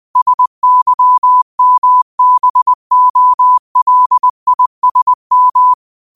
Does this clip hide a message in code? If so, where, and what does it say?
Yes: Morse code from 0.2 to 5.7 s, heard as a keyed tone, about 20 wpm, 1000 Hz. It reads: SYMBOLISM